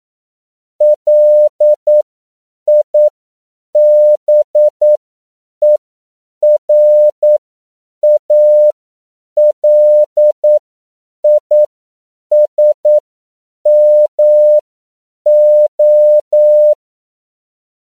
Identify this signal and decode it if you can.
Morse code, heard as a keyed tone, about 9 wpm, spaced as for 7 wpm, 600 Hz. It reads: LIBERALISMO